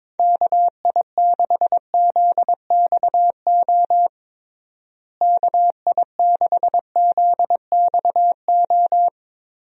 Morse code, heard as a keyed tone, about 22 wpm, 700 Hz. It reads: KI6ZXO KI6ZXO
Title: USA Call Signs - 6 Characters - Repeated 3x - 22wpm